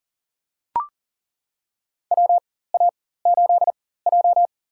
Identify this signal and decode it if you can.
Morse code, heard as a keyed tone, about 37 wpm, spaced as for 18 wpm, 700 Hz. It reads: WA8J
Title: USA Call Signs - 4x Character Spacing - 40wpm